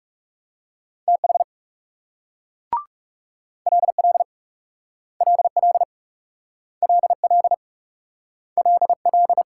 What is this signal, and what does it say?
Morse code, heard as a keyed tone, about 39 wpm, 700 Hz. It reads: TH LL LL LL LL